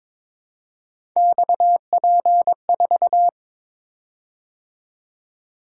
Morse code, heard as a keyed tone, about 22 wpm, 700 Hz. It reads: XP4
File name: practice / 22wpm / Three Letter-Number Combinations - 22wpm